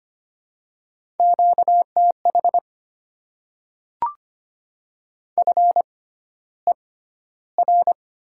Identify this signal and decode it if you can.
Morse code, heard as a keyed tone, about 25 wpm, spaced as for 13 wpm, 700 Hz. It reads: QTH FER
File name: Top 20 QSO Elements - 6x Character Spacing - 25wpm